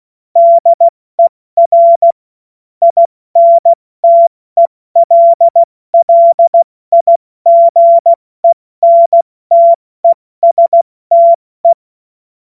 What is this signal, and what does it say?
Morse code, heard as a keyed tone, about 16 wpm, 680 Hz. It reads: DER INTELLIGENTESTE
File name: Morsekurs / Witze / Witz 16. Februar 2021, 70 F80